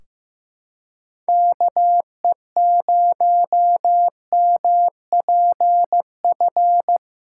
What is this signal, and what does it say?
Morse code, heard as a keyed tone, about 15 wpm, 700 Hz. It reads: KE0MPF
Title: Practice QSO - Exchange 4 - Ending - 15wpm